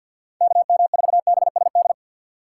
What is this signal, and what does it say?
Morse code, heard as a keyed tone, about 60 wpm, 700 Hz. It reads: KM4BSD
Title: Speed Racing - USA 2x3 Call Signs 50wpm